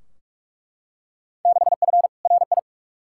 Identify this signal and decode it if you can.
Morse code, heard as a keyed tone, about 42 wpm, 700 Hz. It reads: 6F RI
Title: Field Day Exchanges Encore - 45wpm